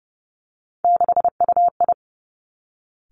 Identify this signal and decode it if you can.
Morse code, heard as a keyed tone, about 30 wpm, 700 Hz. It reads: 6UI
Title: Three Challenging Letter-Number Combinations 30wpm